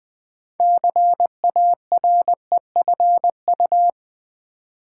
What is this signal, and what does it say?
Morse code, heard as a keyed tone, about 20 wpm, 700 Hz. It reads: CAREFU
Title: Build A Word - 3 Letters At A Time - 9-Letter Words - 20wpm